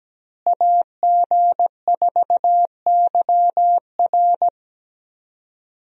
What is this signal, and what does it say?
Morse code, heard as a keyed tone, about 17 wpm, 700 Hz. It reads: AG4YR
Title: USA Call Signs - 5 Characters - Repeated 2x - 17wpm